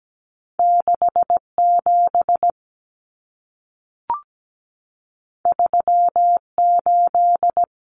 Morse code, heard as a keyed tone, about 17 wpm, 700 Hz. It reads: 67 38
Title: Common Cut Numbers - 2-Digit 17wpm